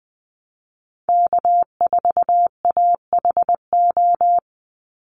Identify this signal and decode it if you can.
Morse code, heard as a keyed tone, about 20 wpm, 700 Hz. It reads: K4AHO
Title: Call Signs Repeated 2x - 20wpm